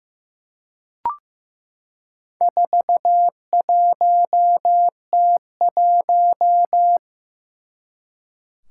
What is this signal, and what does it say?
Morse code, heard as a keyed tone, about 15 wpm, 700 Hz. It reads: 41T1
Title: Four Challenging Letter-Number Combinations 15wpm